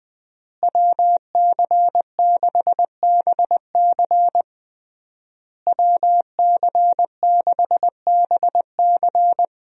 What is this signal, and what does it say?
Morse code, heard as a keyed tone, about 21 wpm, 700 Hz. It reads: WC6BC WC6BC
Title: USA Call Signs - 5 Characters - Repeated 3x - 20wpm